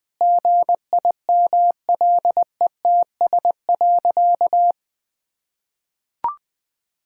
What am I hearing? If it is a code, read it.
Morse code, heard as a keyed tone, about 19 wpm, 700 Hz. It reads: GIMLETS.
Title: Pangram Challenge v2 - 20wpm